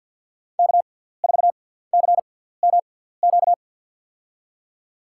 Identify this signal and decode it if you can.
Morse code, heard as a keyed tone, about 48 wpm, spaced as for 17 wpm, 700 Hz. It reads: K4CMQ
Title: USA Call Signs - 5 Characters - 6x Character Spacing - 50wpm